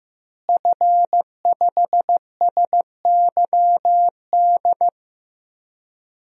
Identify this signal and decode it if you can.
Morse code, heard as a keyed tone, about 15 wpm, 700 Hz. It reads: F5SYD
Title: DXCC-100 Call Signs - 5 Characters - Repeated 2x - 15wpm